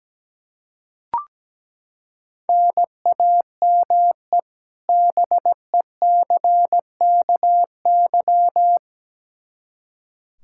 Morse code, heard as a keyed tone, about 17 wpm, 700 Hz. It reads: NAME BECKY
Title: Practice QSO - Exchange 1 - Beginning - 17wpm